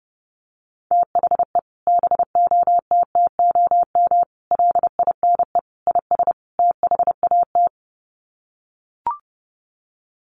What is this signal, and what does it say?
Morse code, heard as a keyed tone, about 30 wpm, 700 Hz. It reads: THE BOTTOM LINE IS THAT
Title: Sets of 5 Words Encore 30wpm